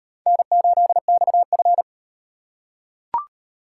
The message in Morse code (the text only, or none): N8XF